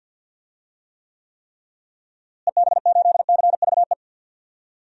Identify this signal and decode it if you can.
Morse code, heard as a keyed tone, about 52 wpm, 700 Hz. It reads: EB8CVE
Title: DXCC-100 Call Signs - 6 Characters - Repeated 3x - 50wpm